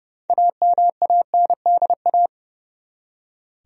Morse code, heard as a keyed tone, about 30 wpm, 700 Hz. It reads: AMANDA
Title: USA Common Names - 30wpm